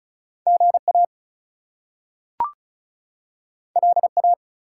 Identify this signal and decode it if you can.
Morse code, heard as a keyed tone, about 36 wpm, 700 Hz. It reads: GA LA
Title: US States and Canadian Provinces 35wpm